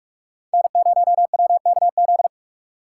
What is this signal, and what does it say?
Morse code, heard as a keyed tone, about 46 wpm, 700 Hz. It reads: N0WKZ